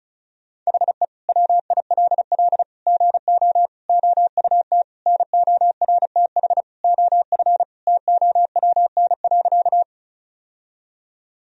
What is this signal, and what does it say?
Morse code, heard as a keyed tone, about 36 wpm, 700 Hz. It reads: HE WILL GO OUT NORTH OF TOWN.
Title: Sentences from Top 1000 Words - Review Part 1 - 35wpm